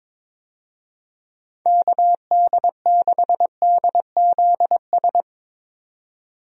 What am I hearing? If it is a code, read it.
Morse code, heard as a keyed tone, about 22 wpm, 700 Hz. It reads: KD6DZS
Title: USA Call Signs - 6 Characters - 22wpm